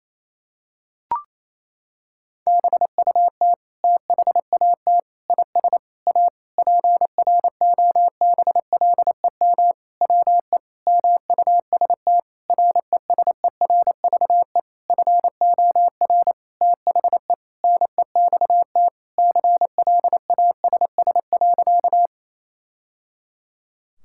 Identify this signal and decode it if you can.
Morse code, heard as a keyed tone, about 28 wpm, 700 Hz. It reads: BUT THAT IS A PROBLEM WE MUST RESERVE FOR THE NEXT CLASS.
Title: Sentences from Top 1000 Words - Review Part 6 - 28wpm